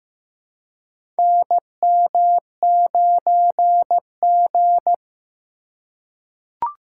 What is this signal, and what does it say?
Morse code, heard as a keyed tone, about 15 wpm, 700 Hz. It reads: NM9G
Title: USA Call Signs - 4 Characters - 3x Character Spacing - 15wpm